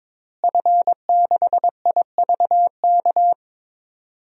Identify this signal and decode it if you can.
Morse code, heard as a keyed tone, about 22 wpm, 700 Hz. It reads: F6IVK
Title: DXCC-100 Call Signs - 5 Characters - Repeated 3x - 22wpm